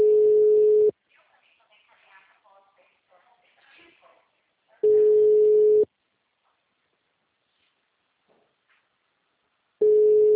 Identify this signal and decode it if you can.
Ringback tone (Europe, 425 Hz), with speech.
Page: Arapongas - PR